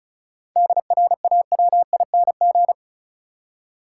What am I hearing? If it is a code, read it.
Morse code, heard as a keyed tone, about 34 wpm, 700 Hz. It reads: DRAWING
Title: Seven Letter Words - 35wpm